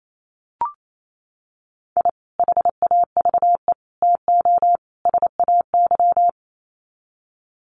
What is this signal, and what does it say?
Morse code, heard as a keyed tone, about 28 wpm, 700 Hz. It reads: I HAVE TO SAY
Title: Sets of 4 Words 28wpm